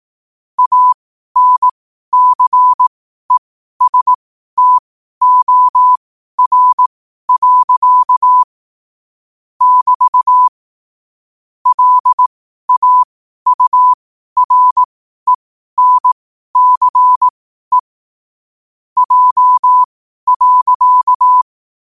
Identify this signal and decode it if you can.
Morse code, heard as a keyed tone, about 18 wpm, spaced as for 12 wpm, 1000 Hz. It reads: ANCESTOR. <BT> LAURENCE J.
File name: qotdfiles / 12wpm